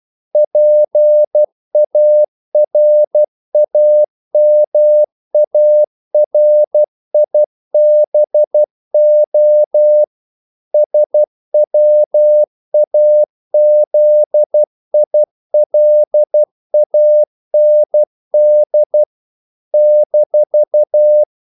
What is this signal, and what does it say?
Morse code, heard as a keyed tone, about 12 wpm, 600 Hz. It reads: PARAMARIBO SWAZILAND -
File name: CW / Practice / NationalCapitals12